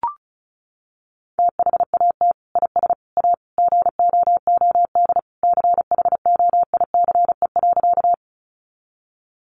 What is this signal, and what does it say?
Morse code, heard as a keyed tone, about 34 wpm, 700 Hz. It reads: THAT IS A GOOD CHOICE.
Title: Sentences from Top 900 Words 35wpm